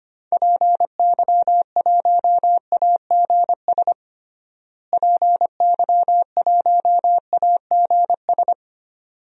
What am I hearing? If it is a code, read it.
Morse code, heard as a keyed tone, about 25 wpm, 700 Hz. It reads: PY1AGS PY1AGS
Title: DXCC-100 Call Signs - 6 Characters - Repeated 2x - 25wpm